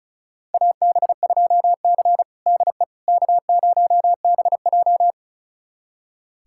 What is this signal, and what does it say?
Morse code, heard as a keyed tone, about 36 wpm, 700 Hz. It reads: AB2C DE K0BJ
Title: USA Callsign DE Callsign - 4 Characters - 35wpm